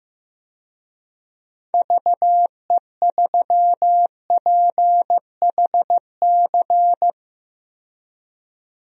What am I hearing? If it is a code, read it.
Morse code, heard as a keyed tone, about 15 wpm, 700 Hz. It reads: VE3PHC
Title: DXCC-100 Call Signs - 6 Characters - 15wpm